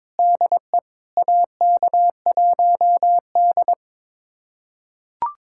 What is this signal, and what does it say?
Morse code, heard as a keyed tone, about 22 wpm, 700 Hz. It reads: DE AK1D
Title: USA Callsign DE Callsign - 4 Characters - 22wpm